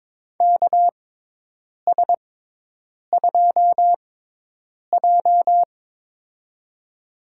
Morse code, heard as a keyed tone, about 21 wpm, spaced as for 8 wpm, 700 Hz. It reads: KS2J
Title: USA Call Signs - 4 Characters - 6x Character Spacing - 22wpm